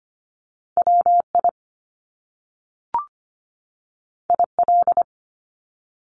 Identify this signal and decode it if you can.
Morse code, heard as a keyed tone, about 25 wpm, 700 Hz. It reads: WI IL
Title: US States 25wpm